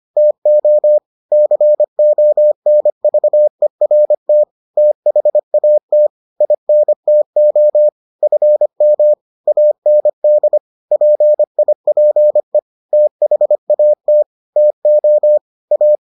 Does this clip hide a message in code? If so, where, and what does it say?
Yes: Morse code from 0.2 to 15.9 s, heard as a keyed tone, about 25 wpm, 600 Hz. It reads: TO CONVERT THAT INTO FM AND PIPE THAT TO A